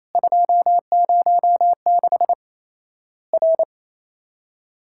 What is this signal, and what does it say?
Morse code, heard as a keyed tone, about 28 wpm, 700 Hz. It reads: 206 R
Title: Worked All Europe DX Contest - QTC Exchange - 28wpm